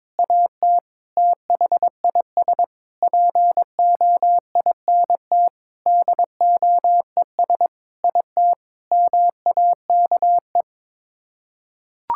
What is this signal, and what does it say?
Morse code, heard as a keyed tone, about 22 wpm, 700 Hz. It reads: AT THIS POINT DOES IT MAKE
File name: practice / 22wpm / Sets of 8 Words - 22wpm